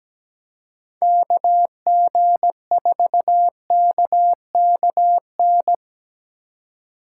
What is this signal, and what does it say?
Morse code, heard as a keyed tone, about 17 wpm, 700 Hz. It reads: KG4KKN
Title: USA Call Signs - 6 Characters - 4x Character Spacing - 17wpm